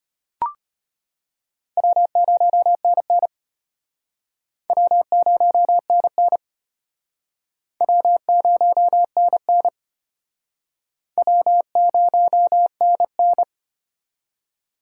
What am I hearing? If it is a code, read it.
Morse code, heard as a keyed tone, about 33 wpm, 700 Hz. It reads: W0NN W0NN W0NN W0NN